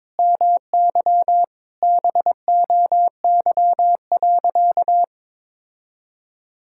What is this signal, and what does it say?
Morse code, heard as a keyed tone, about 22 wpm, 700 Hz. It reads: MY BOY.
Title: Sentences from Top 1000 Words - Review Part 5 - 22wpm